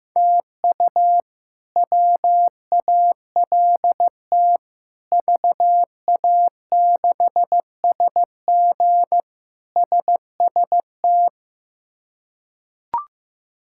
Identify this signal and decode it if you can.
Morse code, heard as a keyed tone, about 15 wpm, 700 Hz. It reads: TU WALT VA6SG SST
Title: K1USN SST QSO Practice - 15wpm